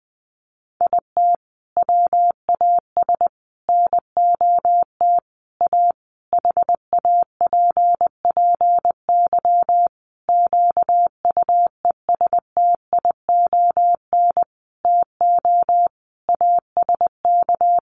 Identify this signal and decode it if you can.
Morse code, heard as a keyed tone, about 20 wpm, 700 Hz. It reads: IT WAS NOT A HAPPY QUESTION TO ASK
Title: Sentences from Top 1000 Words 20wpm